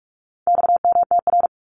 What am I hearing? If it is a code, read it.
Morse code, heard as a keyed tone, about 44 wpm, 700 Hz. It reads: XMTR
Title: Abbreviations 45wpm